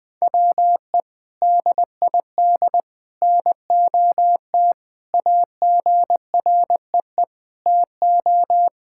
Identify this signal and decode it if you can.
Morse code, heard as a keyed tone, about 20 wpm, 700 Hz. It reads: WE DID NOT AGREE TO
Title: Sentences with 3-Word Phrases - 20wpm